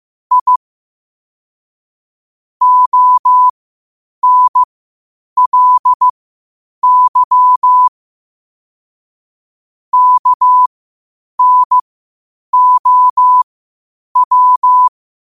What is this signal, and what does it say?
Morse code, heard as a keyed tone, about 15 wpm, spaced as for 8 wpm, 1000 Hz. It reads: I ONLY KNOW
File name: qotdfiles / 8wpm